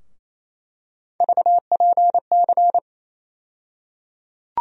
Morse code, heard as a keyed tone, about 28 wpm, 700 Hz. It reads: VPC
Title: USA Call Sign Postfixes - 3 Letters 28wpm